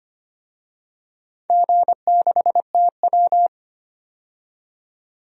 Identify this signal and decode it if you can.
Morse code, heard as a keyed tone, about 25 wpm, 700 Hz. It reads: G6TW
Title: DXCC-100 Call Signs - 4 Characters - 25wpm